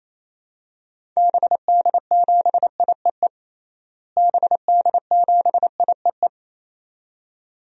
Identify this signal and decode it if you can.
Morse code, heard as a keyed tone, about 27 wpm, 700 Hz. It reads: BD7IEE BD7IEE
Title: DXCC-100 Call Signs - 6 Characters - Repeated 2x - 28wpm